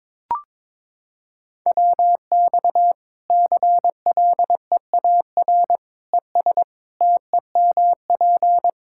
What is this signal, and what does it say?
Morse code, heard as a keyed tone, about 22 wpm, 700 Hz. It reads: WX CLEAR ES TEMP
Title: Practice QSO - Exchange 2 - Info - 22wpm